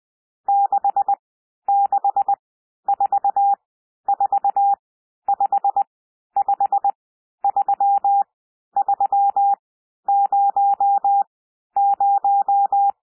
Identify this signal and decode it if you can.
Morse code, heard as a keyed tone, about 20 wpm, spaced as for 11 wpm, 800 Hz. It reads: 6644553300
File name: cwmp3 / N1 SOUND NUMBERS 20 wpm